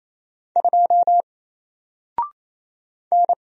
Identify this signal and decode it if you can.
Morse code, heard as a keyed tone, about 27 wpm, 700 Hz. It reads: 2 N
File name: Single-Letter-Number-28wpm